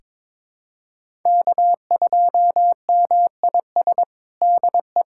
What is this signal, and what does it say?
Morse code, heard as a keyed tone, about 22 wpm, 700 Hz. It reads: K2MIS DE
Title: USA Callsign DE Callsign - 5 Characters - 22wpm